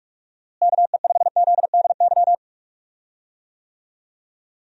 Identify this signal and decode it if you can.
Morse code, heard as a keyed tone, about 45 wpm, 700 Hz. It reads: KE5ZDY
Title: USA Call Signs - 6 Characters - Repeated 3x - 45wpm